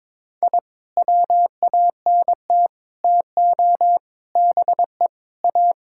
Morse code, heard as a keyed tone, about 22 wpm, 700 Hz. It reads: I WANT TO BE A
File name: practice / 22wpm / Sentences with 3-Word Phrases - 22wpm